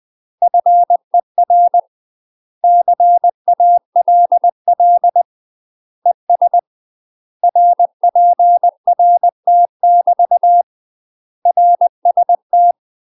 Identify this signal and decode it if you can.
Morse code, heard as a keyed tone, about 20 wpm, 700 Hz. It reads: FER CALL ES RPRT <BT> RST
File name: MASTER QSOs TRAINING/20WPM